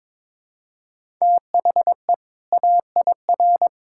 Morse code, heard as a keyed tone, about 21 wpm, 700 Hz. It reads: THE AIR
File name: Build A Phrase - 2-Word Phrases - 22wpm